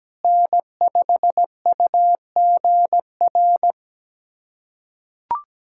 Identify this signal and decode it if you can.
Morse code, heard as a keyed tone, about 17 wpm, 700 Hz. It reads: N5UGR
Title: USA Callsign DE Callsign - 5 Characters - 17wpm